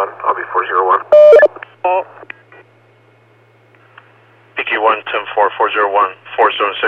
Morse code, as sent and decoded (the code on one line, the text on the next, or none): none